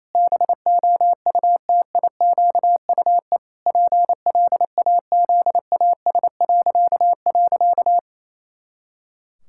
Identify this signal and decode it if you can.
Morse code, heard as a keyed tone, about 28 wpm, 700 Hz. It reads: BOUTIQUE PLAZAS..
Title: Pangram Challenge v2 - 28wpm